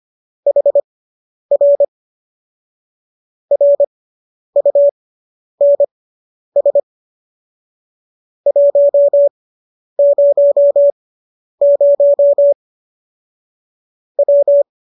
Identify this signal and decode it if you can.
Morse code, heard as a keyed tone, about 25 wpm, spaced as for 10 wpm, 570 Hz. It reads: HR RUNS 100 W